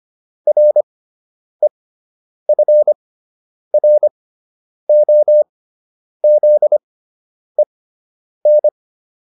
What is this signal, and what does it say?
Morse code, heard as a keyed tone, about 25 wpm, spaced as for 9 wpm, 600 Hz. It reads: REFROZEN